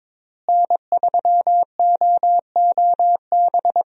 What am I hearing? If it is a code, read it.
Morse code, heard as a keyed tone, about 22 wpm, 700 Hz. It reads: N3OOB
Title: USA Call Signs - 5 Characters - Repeated 3x - 22wpm